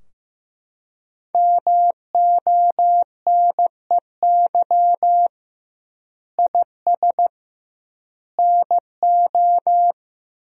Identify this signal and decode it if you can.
Morse code, heard as a keyed tone, about 15 wpm, 700 Hz. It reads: MONEY IS NO
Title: 2x Word-Spacing - Sentences from Top 500 Words - 15wpm